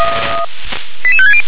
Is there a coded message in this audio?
Yes: five-tone selective call (ZVEI) from 1.0 to 1.4 s, heard as quick runs of five tones, tones of about 70 ms. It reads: 88560